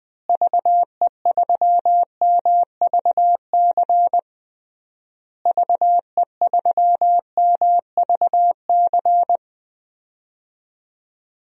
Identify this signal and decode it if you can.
Morse code, heard as a keyed tone, about 20 wpm, 700 Hz. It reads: VE3MVC VE3MVC
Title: DXCC-100 Call Signs - 6 Characters - Repeated 3x - 20wpm